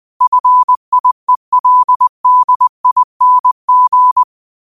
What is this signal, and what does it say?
Morse code, heard as a keyed tone, about 20 wpm, 1000 Hz. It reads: FIELDING
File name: qotdfiles / 20wpm